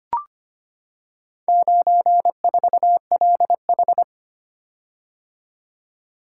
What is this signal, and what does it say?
Morse code, heard as a keyed tone, about 25 wpm, 700 Hz. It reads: 94LH
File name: practice / 25wpm / Four Letter-Number Combinations - 25wpm